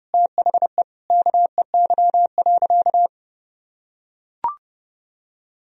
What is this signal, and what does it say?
Morse code, heard as a keyed tone, about 30 wpm, 700 Hz. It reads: THE KEY.
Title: Sentences from Top 1000 Words - Review Part 4 - 30wpm